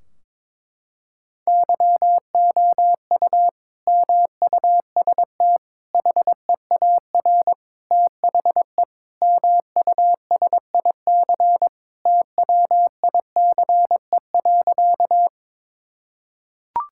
Morse code, heard as a keyed tone, about 22 wpm, 700 Hz. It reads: YOU MUST HEAR THE MUSIC TWICE.
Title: Sentences from Top 1600 Words 22wpm